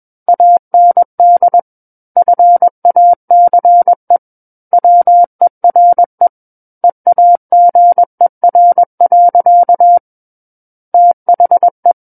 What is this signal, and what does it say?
Morse code, heard as a keyed tone, about 21 wpm, 700 Hz. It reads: AND FACE WERE EAGER. THE
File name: Book One / 12-Chapter 12 - 21 WPM